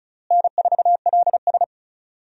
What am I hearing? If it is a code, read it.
Morse code, heard as a keyed tone, about 34 wpm, 700 Hz. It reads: N4LS
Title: USA Call Signs - 4 Characters - Repeated 3x - 35wpm